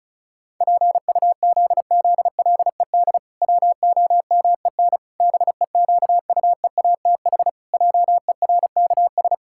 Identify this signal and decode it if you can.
Morse code, heard as a keyed tone, about 36 wpm, 700 Hz. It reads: PUZZLED WOMEN BEQUEATH JERKS
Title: Pangram Challenge v2 - 35wpm